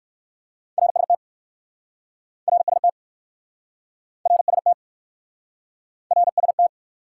Morse code, heard as a keyed tone, about 59 wpm, 700 Hz. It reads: RST RST RST RST